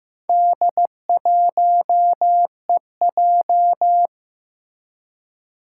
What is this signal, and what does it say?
Morse code, heard as a keyed tone, about 16 wpm, 700 Hz. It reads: D1EJ
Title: Four Challenging Letter-Number Combinations 15wpm